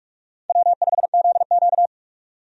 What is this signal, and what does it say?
Morse code, heard as a keyed tone, about 46 wpm, 700 Hz. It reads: W5ZQ
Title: USA Call Signs - 4 Characters - 45wpm